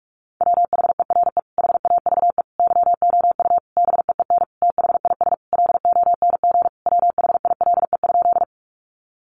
Morse code, heard as a keyed tone, about 44 wpm, 700 Hz. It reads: WHERE HAVE YOU BEEN THIS LONG WHILE?
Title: Sentences from Top 300 Words 45wpm